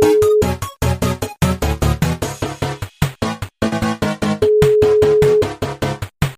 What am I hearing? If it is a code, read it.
Ringback tone (Europe, 425 Hz), with noise.